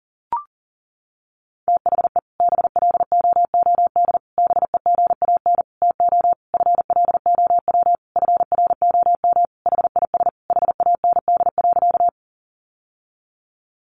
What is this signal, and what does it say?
Morse code, heard as a keyed tone, about 40 wpm, 700 Hz. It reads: THE BLOOD BEGAN TO FLOW FROM HIS HAND.
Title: Sentences from Top 1200 Words 40wpm